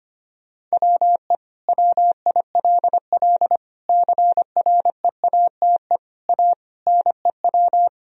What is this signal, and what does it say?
Morse code, heard as a keyed tone, about 25 wpm, 700 Hz. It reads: WE WILL CREATE A NEW
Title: Sentences from Top 1000 Words - Review Part 3 - 25wpm